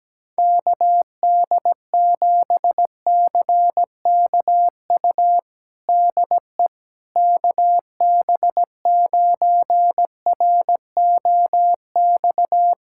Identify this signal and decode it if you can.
Morse code, heard as a keyed tone, about 17 wpm, 700 Hz. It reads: KD7CKU DE KB9ROX
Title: USA Callsign DE Callsign 17wpm